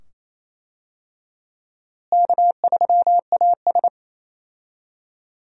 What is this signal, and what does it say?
Morse code, heard as a keyed tone, about 28 wpm, 700 Hz. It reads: K3AS
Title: USA Call Signs - 4 Characters - Repeated 2x - 28wpm